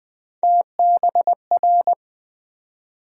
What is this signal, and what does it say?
Morse code, heard as a keyed tone, about 20 wpm, 700 Hz. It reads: TBR
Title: USA Call Sign Postfixes - 3 Letters 20wpm